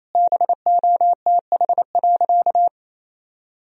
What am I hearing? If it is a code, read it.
Morse code, heard as a keyed tone, about 28 wpm, 700 Hz. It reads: BOTH.
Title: Sentences from Top 1000 Words - Review Part 1 - 28wpm